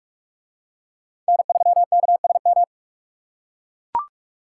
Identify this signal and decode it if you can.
Morse code, heard as a keyed tone, about 44 wpm, 700 Hz. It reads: N3KSM